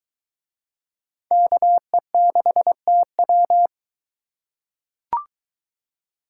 Morse code, heard as a keyed tone, about 21 wpm, 700 Hz. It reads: KE6TW